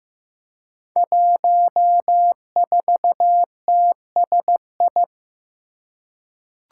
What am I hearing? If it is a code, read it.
Morse code, heard as a keyed tone, about 15 wpm, 700 Hz. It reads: 14TSI
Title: Five Challenging Letter-Number Combinations 15wpm